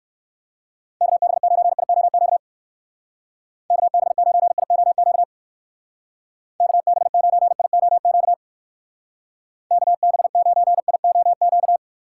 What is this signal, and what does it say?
Morse code, heard as a keyed tone, about 59 wpm, 700 Hz. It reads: KB9IOQ KB9IOQ KB9IOQ KB9IOQ